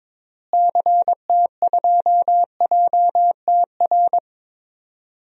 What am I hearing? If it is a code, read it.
Morse code, heard as a keyed tone, about 22 wpm, 700 Hz. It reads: CT2JTR
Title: DXCC-100 Call Signs - 6 Characters - Repeated 3x - 22wpm